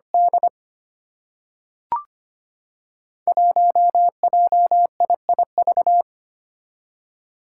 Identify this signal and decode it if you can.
Morse code, heard as a keyed tone, about 25 wpm, 700 Hz. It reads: D 1JIIV
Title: Five Challenging Letter-Number Combinations 25wpm